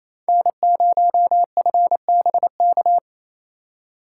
Morse code, heard as a keyed tone, about 28 wpm, 700 Hz. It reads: N0FBK
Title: USA Call Signs - 3x Character Spacing - 28wpm